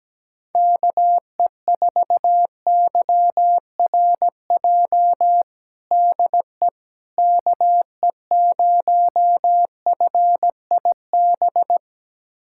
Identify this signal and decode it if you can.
Morse code, heard as a keyed tone, about 17 wpm, 700 Hz. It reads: KE4YRJ DE KE0FIB
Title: USA Callsign DE Callsign - 6 Characters - 17wpm